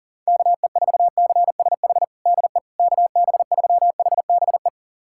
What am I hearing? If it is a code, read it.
Morse code, heard as a keyed tone, about 40 wpm, 700 Hz. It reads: KE4CSH DE KB3HBE
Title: USA Callsign DE Callsign 40wpm